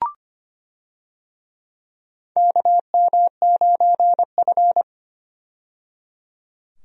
Morse code, heard as a keyed tone, about 25 wpm, 700 Hz. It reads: KM9F
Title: USA Call Signs - 4 Characters - 25wpm